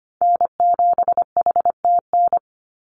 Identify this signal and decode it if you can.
Morse code, heard as a keyed tone, about 25 wpm, 700 Hz. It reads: N7HTN
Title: USA Call Signs 25wpm